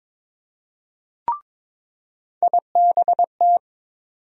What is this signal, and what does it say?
Morse code, heard as a keyed tone, about 22 wpm, 700 Hz. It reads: IBT
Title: USA Call Sign Postfixes - 3 Letters 22wpm